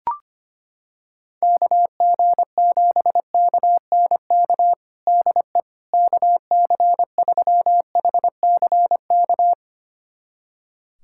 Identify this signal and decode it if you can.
Morse code, heard as a keyed tone, about 25 wpm, 700 Hz. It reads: KG7KNK DE KC3HCK
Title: USA Callsign DE Callsign - 6 Characters - 25wpm